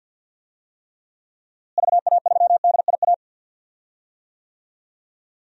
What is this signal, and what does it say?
Morse code, heard as a keyed tone, about 53 wpm, 700 Hz. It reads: VA3DIA
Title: DXCC-100 Call Signs - 6 Characters - Repeated 2x - 50wpm